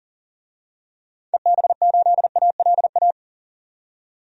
Morse code, heard as a keyed tone, about 40 wpm, 700 Hz. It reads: EB8ALA
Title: DXCC-100 Call Signs - 6 Characters - 40wpm